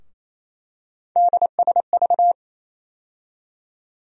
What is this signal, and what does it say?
Morse code, heard as a keyed tone, about 28 wpm, 700 Hz. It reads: DSV
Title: USA Call Sign Postfixes - 3 Letters 28wpm